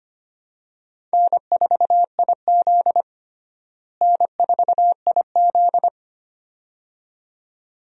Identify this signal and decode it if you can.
Morse code, heard as a keyed tone, about 25 wpm, 700 Hz. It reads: N4IZ N4IZ
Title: USA Call Signs - 4 Characters - Repeated 2x - 25wpm